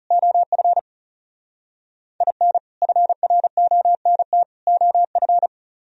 Morse code, heard as a keyed tone, about 34 wpm, 700 Hz. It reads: OF IN FRONT OF
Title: Speed-Racing - Sets of 3 Words 30wpm